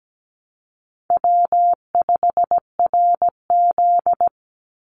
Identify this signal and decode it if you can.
Morse code, heard as a keyed tone, about 17 wpm, 700 Hz. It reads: W5RZ
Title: Easy Call Signs Repeated 2x - 17wpm